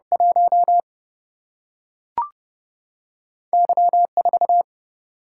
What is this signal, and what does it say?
Morse code, heard as a keyed tone, about 30 wpm, 700 Hz. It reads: 1 Y 4
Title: Single-Letter Single-Number 30wpm